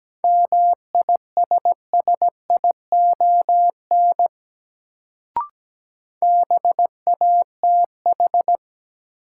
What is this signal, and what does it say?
Morse code, heard as a keyed tone, about 17 wpm, 700 Hz. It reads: MISSION BATH
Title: Sentences from Top 1600 Words - New Word Drill - 17wpm